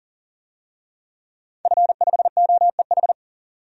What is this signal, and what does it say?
Morse code, heard as a keyed tone, about 40 wpm, 700 Hz. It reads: F5OEH
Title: DXCC-100 Call Signs - 5 Characters - 40wpm